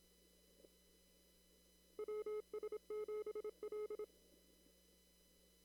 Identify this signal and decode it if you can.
Morse code, heard as a keyed tone, about 26 wpm, 1280 Hz. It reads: WS7L
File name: CW / kt5j wpx cw-op-k5pi